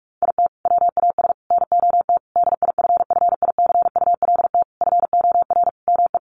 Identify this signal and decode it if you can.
Morse code, heard as a keyed tone, about 44 wpm, 700 Hz. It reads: IT WAS NOT DIFFICULT FOR ME
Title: Sentences from Top 1200 Words 45wpm